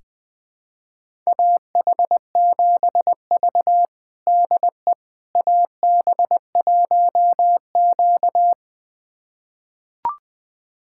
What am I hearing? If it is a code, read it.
Morse code, heard as a keyed tone, about 20 wpm, 700 Hz. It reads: AH7V DE AB1Q
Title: USA Callsign DE Callsign - 4 Characters - 20wpm